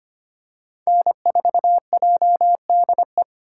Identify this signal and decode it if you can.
Morse code, heard as a keyed tone, about 25 wpm, 700 Hz. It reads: N4JDE
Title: USA Call Signs - 2x Character Spacing - 25wpm